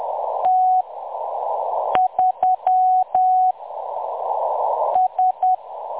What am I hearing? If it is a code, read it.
Morse code, heard as a keyed tone, about 10 wpm, spaced as for 6 wpm, 740 Hz. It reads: T3 S
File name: M23-ST3- -0858utc-10184khz